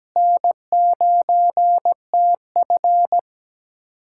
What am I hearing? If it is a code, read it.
Morse code, heard as a keyed tone, about 17 wpm, 700 Hz. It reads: N9TF
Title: USA Callsign DE Callsign - 4 Characters - 17wpm